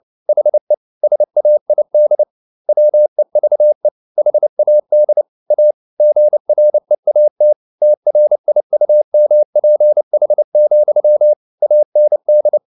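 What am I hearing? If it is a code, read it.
Morse code, heard as a keyed tone, about 29 wpm, 600 Hz. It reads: HE SAID WEVE HAD A GREAT TRIUMPH, AND
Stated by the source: Morse code at 29 wpm, sending Ken Follett's novel Never in English, from CW med Gnister.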